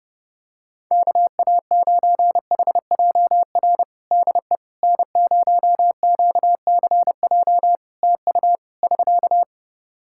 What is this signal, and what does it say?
Morse code, heard as a keyed tone, about 30 wpm, 700 Hz. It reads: KA9HJR DE N0QCJ TU <SK>
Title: Practice QSO - Exchange 4 - Ending - 30wpm